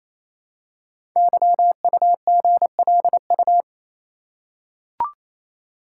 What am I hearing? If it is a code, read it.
Morse code, heard as a keyed tone, about 27 wpm, 700 Hz. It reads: YUGLU
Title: Code Groups - 5 Letter:Numbers - 28wpm